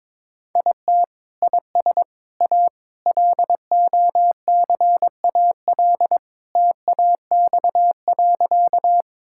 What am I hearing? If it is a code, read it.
Morse code, heard as a keyed tone, about 22 wpm, 700 Hz. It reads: IT IS A LOCAL TAX.
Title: Sentences from Top 1000 Words - Review Part 5 - 22wpm